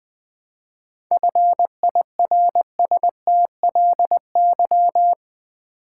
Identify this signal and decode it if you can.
Morse code, heard as a keyed tone, about 20 wpm, 700 Hz. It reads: FIRSTLY
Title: 7 Letter Words - First 3 Letters Given - 20wpm